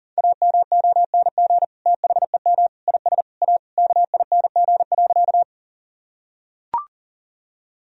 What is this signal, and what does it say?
Morse code, heard as a keyed tone, about 40 wpm, 700 Hz. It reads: AMONG THEM IS A KING.
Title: Sentences from Top 1000 Words - Review Part 3 - 40wpm